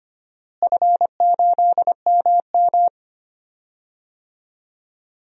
Morse code, heard as a keyed tone, about 25 wpm, 700 Hz. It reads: F8MM
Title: DXCC-100 Call Signs - 4 Characters - Repeated 2x - 25wpm